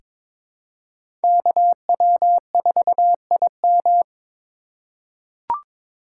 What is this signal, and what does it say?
Morse code, heard as a keyed tone, about 22 wpm, 700 Hz. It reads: KW4IM
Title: USA Call Signs - 5 Characters - 2x Character Spacing - 22wpm